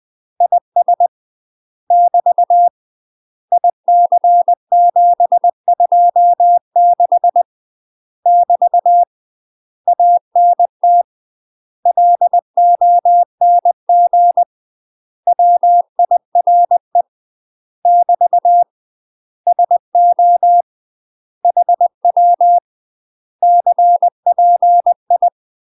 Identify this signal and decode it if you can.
Morse code, heard as a keyed tone, about 20 wpm, 700 Hz. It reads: IS <BT> IC726 <BT> ANT LONG WIRE <BT> SO HW CPI
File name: UK MASTER QSOs TRAINING / 20WPM / UK MASTER QSOs 0046